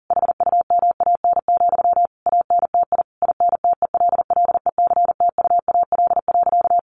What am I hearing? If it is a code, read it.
Morse code, heard as a keyed tone, about 40 wpm, 700 Hz. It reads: HUMAN, ANTI INTELLECTUAL.
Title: A Sound Of Thunder 40wpm